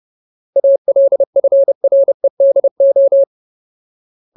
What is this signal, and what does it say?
Morse code, heard as a keyed tone, about 30 wpm, 550 Hz. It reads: ALFREDO
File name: Ultimate Word List / 30WPM Ultimate